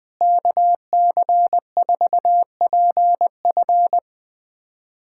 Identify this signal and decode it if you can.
Morse code, heard as a keyed tone, about 21 wpm, 700 Hz. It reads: KC4PF
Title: USA Callsign DE Callsign - 5 Characters - 20wpm